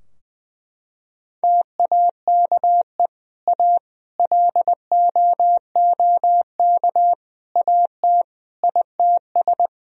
Morse code, heard as a keyed tone, about 21 wpm, 700 Hz. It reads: TAKE A LOOK AT ITS
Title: Sentences from Top 1000 Words - Review Part 2 - 20wpm